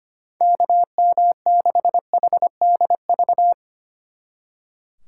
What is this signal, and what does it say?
Morse code, heard as a keyed tone, about 26 wpm, 700 Hz. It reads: KM6HDV
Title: Speed Racing - USA 2x3 Call Signs 25wpm